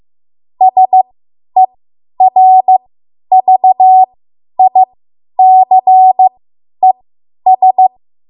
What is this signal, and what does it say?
Morse code, heard as a keyed tone, about 15 wpm, spaced as for 10 wpm, 750 Hz. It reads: SERVICES
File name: Morse / 10WPM